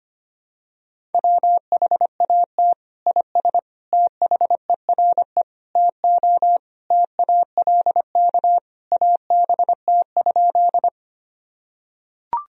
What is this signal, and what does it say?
Morse code, heard as a keyed tone, about 25 wpm, 700 Hz. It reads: WHAT IS THERE TO TALK ABT?
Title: Sentences from Top 400 Words and Abbreviations 25wpm